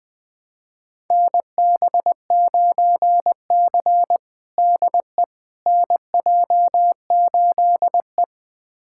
Morse code, heard as a keyed tone, about 20 wpm, 700 Hz. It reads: NB9C DE NJ8E
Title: USA Callsign DE Callsign - 4 Characters - 20wpm